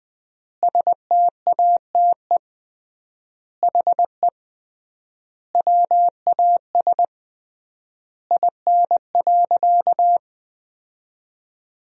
Morse code, heard as a keyed tone, about 20 wpm, 700 Hz. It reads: STATE HE WAS IN.
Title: 3x Word-Spacing - Sentences from Top 400 Words - 20wpm